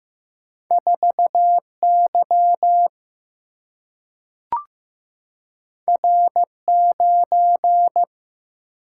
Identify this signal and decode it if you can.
Morse code, heard as a keyed tone, about 15 wpm, 700 Hz. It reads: 4Y R9
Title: Single-Letter Single-Number 15wpm